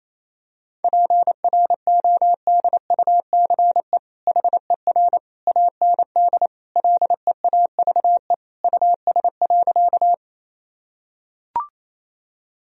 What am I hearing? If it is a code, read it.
Morse code, heard as a keyed tone, about 28 wpm, 700 Hz. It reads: PRODUCE HER AND LEAVE US.
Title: Sentences from Top 1000 Words - Review Part 3 - 28wpm